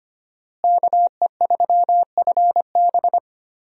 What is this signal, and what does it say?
Morse code, heard as a keyed tone, about 25 wpm, 700 Hz. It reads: KE3FB
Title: USA Call Signs - 5 Characters - Repeated 3x - 25wpm